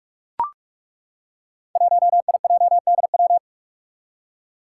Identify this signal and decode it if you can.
Morse code, heard as a keyed tone, about 41 wpm, 700 Hz. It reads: 1IJDW
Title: Five Challenging Letter-Number Combinations 45wpm